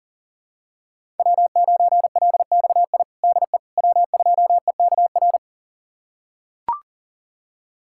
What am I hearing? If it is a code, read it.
Morse code, heard as a keyed tone, about 40 wpm, 700 Hz. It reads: W9LXI DE W2EKR
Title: USA Callsign DE Callsign - 5 Characters - 40wpm